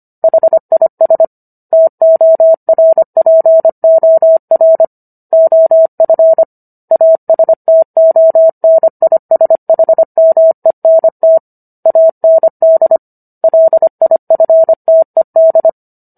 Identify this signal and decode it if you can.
Morse code, heard as a keyed tone, about 25 wpm, 650 Hz. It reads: HIS TORPOR OF ASTONISHMENT AND LIFTED